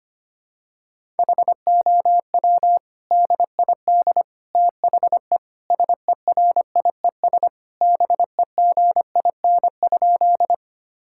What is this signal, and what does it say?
Morse code, heard as a keyed tone, about 25 wpm, 700 Hz. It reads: HOW DID THE SERIES BEGIN?
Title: Sentences from Top 1000 Words - Review Part 5 - 25wpm